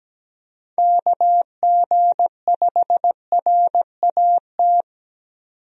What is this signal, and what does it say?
Morse code, heard as a keyed tone, about 17 wpm, 700 Hz. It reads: KG5RAT
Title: USA Call Signs - 6 Characters - 17wpm